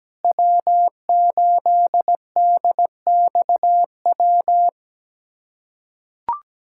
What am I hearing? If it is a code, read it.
Morse code, heard as a keyed tone, about 17 wpm, 700 Hz. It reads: W8DXW
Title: USA Callsign DE Callsign - 5 Characters - 17wpm